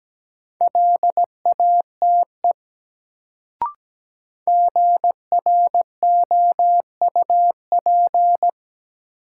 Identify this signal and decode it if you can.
Morse code, heard as a keyed tone, about 17 wpm, 700 Hz. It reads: LATE GROUP
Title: Sets of 3 Words - New Word Drill - 17wpm